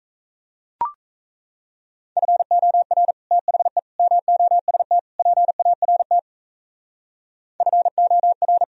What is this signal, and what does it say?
Morse code, heard as a keyed tone, about 35 wpm, 700 Hz. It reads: FOR THE MOST PART FOR
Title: Speed-Racing - Sets of 4 Words 28wpm